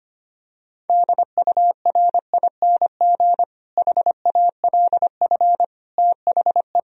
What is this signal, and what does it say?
Morse code, heard as a keyed tone, about 25 wpm, 700 Hz. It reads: DURING HALF THE
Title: Sentences from Top 1000 Words - Review Part 2 - 25wpm